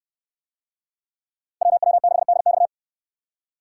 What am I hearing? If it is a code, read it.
Morse code, heard as a keyed tone, about 65 wpm, 700 Hz. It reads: WW6DX